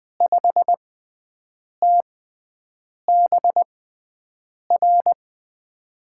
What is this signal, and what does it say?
Morse code, heard as a keyed tone, about 20 wpm, spaced as for 7 wpm, 700 Hz. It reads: 5TBR
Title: USA Call Signs - 5 Characters - 6x Character Spacing - 20wpm